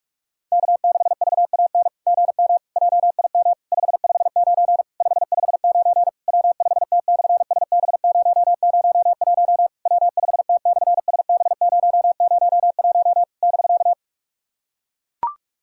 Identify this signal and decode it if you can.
Morse code, heard as a keyed tone, about 44 wpm, 690 Hz. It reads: K6VAN GM JIM 559 559 W5T/SB001 W5T/SB001 <BK>
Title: SOTA QSO Practice 45wpm